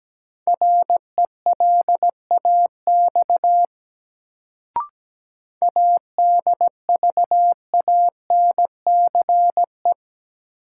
Morse code, heard as a keyed tone, about 17 wpm, 700 Hz. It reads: RELAX ADVANCE
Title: Sentences from Top 1400 Words - New Word Drill - 17wpm